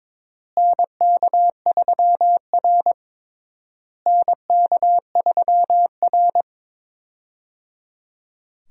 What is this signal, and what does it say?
Morse code, heard as a keyed tone, about 21 wpm, 700 Hz. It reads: NK3R NK3R
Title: USA Call Signs - 4 Characters - Repeated 3x - 22wpm